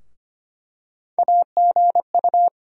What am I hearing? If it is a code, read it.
Morse code, heard as a keyed tone, about 25 wpm, 700 Hz. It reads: AGU
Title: USA Call Sign Postfixes - 3 Letters 25wpm